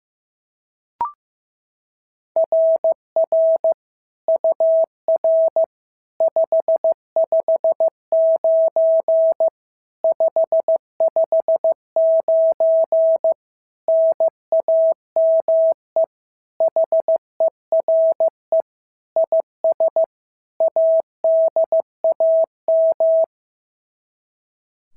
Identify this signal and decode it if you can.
Morse code, heard as a keyed tone, about 15 wpm, 650 Hz. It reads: RR UR 559 559 NAME HERE IS ADAM
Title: SOTA QSO Practice 15wpm